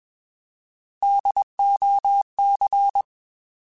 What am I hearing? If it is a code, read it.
Morse code, heard as a keyed tone, about 21 wpm, 800 Hz. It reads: DOC
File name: Alphabetical / Alphabetical mp3s / Ds @20